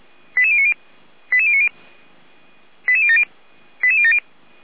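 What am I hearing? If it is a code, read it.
Five-tone selective call (ZVEI), heard as quick runs of five tones, tones of about 70 ms. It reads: 88090 88090 88070 88070
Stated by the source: ZVEI five-tone call received by SDR.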